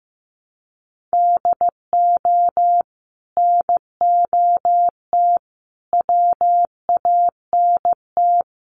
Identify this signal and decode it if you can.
Morse code, heard as a keyed tone, about 15 wpm, 700 Hz. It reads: DO NOT WANT
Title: Sets of 5 Words Encore 15wpm